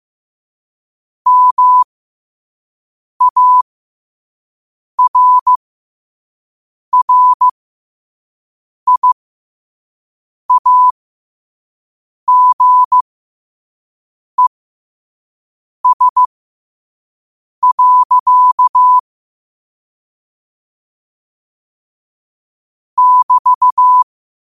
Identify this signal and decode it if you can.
Morse code, heard as a keyed tone, about 15 wpm, spaced as for 5 wpm, 1000 Hz. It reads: MARRIAGES. <BT>
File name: qotdfiles / 5wpm